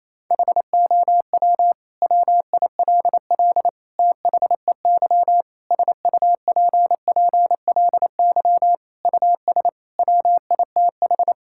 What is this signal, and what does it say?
Morse code, heard as a keyed tone, about 28 wpm, 700 Hz. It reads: HOW WILL THEY SUPPLY US WITH
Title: Sentences from Top 1000 Words - Review Part 5 - 28wpm